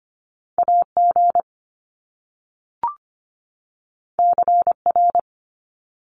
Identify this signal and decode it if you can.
Morse code, heard as a keyed tone, about 25 wpm, 700 Hz. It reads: AG CR
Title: Top Two Letter Combinations in Words 25wpm